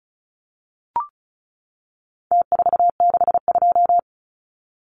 Morse code, heard as a keyed tone, about 34 wpm, 700 Hz. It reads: T462
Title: Numbers - Contest Style - 4-Digit 35wpm